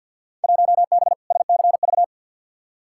Morse code, heard as a keyed tone, about 46 wpm, 700 Hz. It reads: 1B SCV
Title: Field Day Exchanges - 50wpm